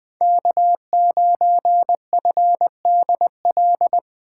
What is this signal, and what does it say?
Morse code, heard as a keyed tone, about 20 wpm, 700 Hz. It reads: K9FDL